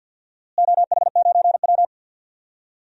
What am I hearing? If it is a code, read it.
Morse code, heard as a keyed tone, about 49 wpm, 700 Hz. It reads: OH9W
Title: DXCC-100 Call Signs - 4 Characters - Repeated 2x - 50wpm